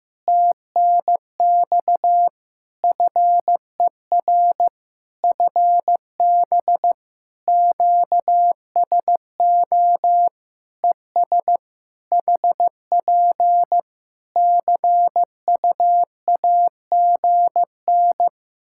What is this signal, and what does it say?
Morse code, heard as a keyed tone, about 15 wpm, 700 Hz. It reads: TNX FER FB QSO ES HP CUAGN
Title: Practice QSO - Exchange 4 - Ending - 15wpm